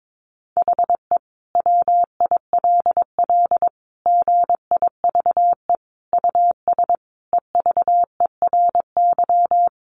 Morse code, heard as a keyed tone, about 22 wpm, 700 Hz. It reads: HE WILL GIVE US EVERY
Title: Sentences from Top 1500 Words 22wpm